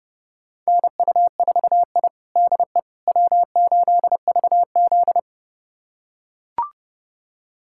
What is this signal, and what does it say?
Morse code, heard as a keyed tone, about 30 wpm, 700 Hz. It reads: NU4I DE W8VZ
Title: USA Callsign DE Callsign - 4 Characters - 30wpm